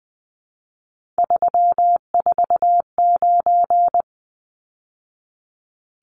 Morse code, heard as a keyed tone, about 20 wpm, 700 Hz. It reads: 349
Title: Numbers - 3-Digit 20wpm